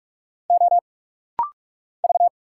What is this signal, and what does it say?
Morse code, heard as a keyed tone, about 46 wpm, 700 Hz. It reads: O V
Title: Single Letter-Number Alt - Rapid-Fire 45wpm